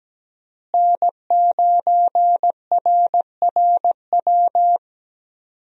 Morse code, heard as a keyed tone, about 17 wpm, 700 Hz. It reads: N9RRW
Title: USA Call Signs - 5 Characters - Repeated 2x - 17wpm